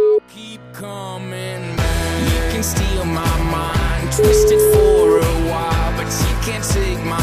Ringback tone (Europe, 425 Hz), with noise.